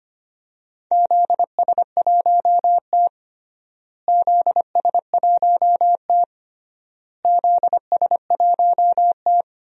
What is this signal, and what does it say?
Morse code, heard as a keyed tone, about 25 wpm, 700 Hz. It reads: ZS1T ZS1T ZS1T
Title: DXCC-100 Call Signs - 4 Characters - Repeated 3x - 25wpm